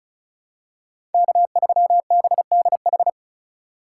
Morse code, heard as a keyed tone, about 36 wpm, 700 Hz. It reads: K3BDH
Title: USA Call Signs - 5 Characters - 4x Character Spacing - 35wpm